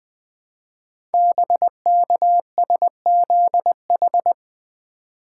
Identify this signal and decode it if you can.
Morse code, heard as a keyed tone, about 20 wpm, 700 Hz. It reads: BKSZH
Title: Code Groups - 5 Letters - 20wpm